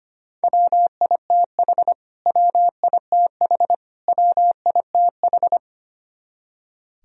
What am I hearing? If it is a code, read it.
Morse code, heard as a keyed tone, about 25 wpm, 700 Hz. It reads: WITH WITH WITH
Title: Top 100 Words Repeated 25wpm